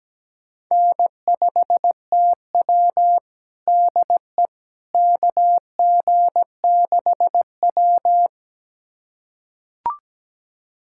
Morse code, heard as a keyed tone, about 17 wpm, 700 Hz. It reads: N5TW DE KG6W
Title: USA Callsign DE Callsign - 4 Characters - 17wpm